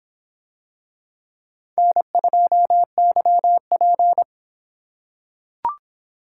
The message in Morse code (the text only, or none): N2YP